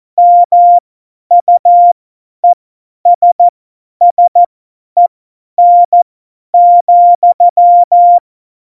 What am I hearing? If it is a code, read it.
Morse code, heard as a keyed tone, about 14 wpm, spaced as for 10 wpm, 700 Hz. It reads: MUESSEN,
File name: Morsekurs / Witze / Witz 06. Oktober 2020, 50 F70